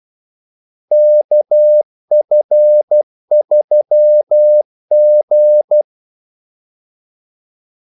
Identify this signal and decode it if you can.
Morse code, heard as a keyed tone, about 12 wpm, 600 Hz. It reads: KF3G
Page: 2 ekstra ordmellemrum mellem kaldesignalerne